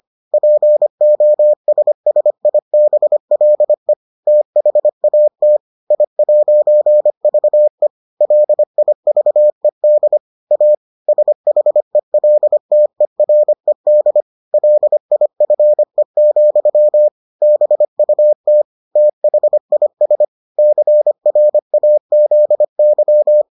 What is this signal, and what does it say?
Morse code, heard as a keyed tone, about 25 wpm, 600 Hz. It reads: POSSIBLE THAT I'VE LIVED A SHELTERED LIFE, BUT THIS CRAZY